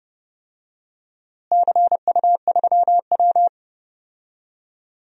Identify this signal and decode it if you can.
Morse code, heard as a keyed tone, about 30 wpm, 700 Hz. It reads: CU3W
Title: DXCC-100 Call Signs - 4 Characters - Repeated 3x - 30wpm